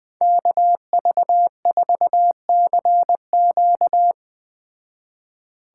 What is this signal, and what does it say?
Morse code, heard as a keyed tone, about 20 wpm, 700 Hz. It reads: KV4CQ
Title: Speed Racing - USA Call Signs 17wpm